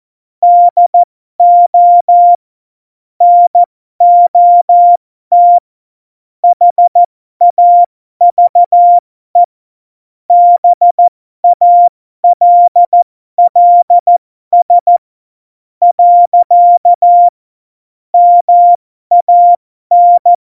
Morse code, heard as a keyed tone, about 14 wpm, spaced as for 11 wpm, 700 Hz. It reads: DO NOT HAVE BALLS. MAN
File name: Morsekurs / Witze